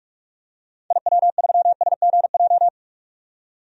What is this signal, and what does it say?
Morse code, heard as a keyed tone, about 46 wpm, 700 Hz. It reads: IW3SGJ
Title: DXCC-100 Call Signs - 6 Characters - 45wpm